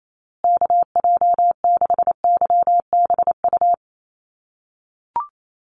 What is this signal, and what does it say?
Morse code, heard as a keyed tone, about 28 wpm, 700 Hz. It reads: KJ6YBU
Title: USA Call Signs Repeated 2x 28wpm